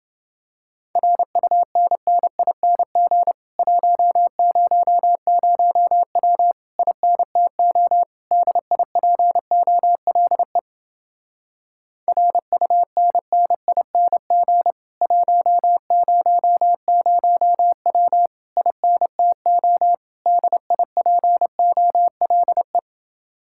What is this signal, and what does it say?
Morse code, heard as a keyed tone, about 29 wpm, 700 Hz. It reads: RUNNING 100W INTO DIPOLE RUNNING 100W INTO DIPOLE